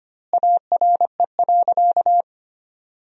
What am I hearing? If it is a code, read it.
Morse code, heard as a keyed tone, about 25 wpm, 700 Hz. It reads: ARE.
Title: 2x Word-Spacing - Sentences from Top 100 Words - 25wpm